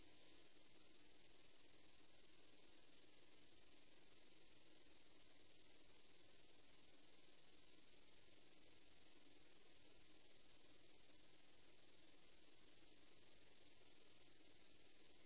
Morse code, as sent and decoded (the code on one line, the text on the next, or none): none